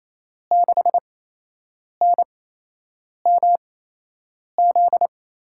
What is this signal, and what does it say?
Morse code, heard as a keyed tone, about 28 wpm, spaced as for 8 wpm, 700 Hz. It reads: 6NMZ
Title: USA Call Signs - 6 Characters - 8x Character Spacing - 28wpm